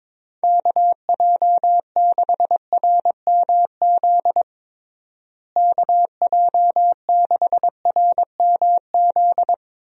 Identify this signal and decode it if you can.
Morse code, heard as a keyed tone, about 22 wpm, 700 Hz. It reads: KJ6RMZ KJ6RMZ
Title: USA Call Signs - 6 Characters - Repeated 3x - 22wpm